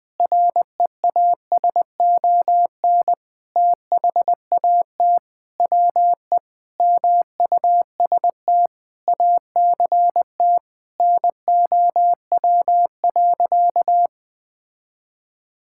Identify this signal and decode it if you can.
Morse code, heard as a keyed tone, about 21 wpm, 700 Hz. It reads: REASON THAT WE MUST ACT NOW.
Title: Sentences with 6-Word Phrases - 20wpm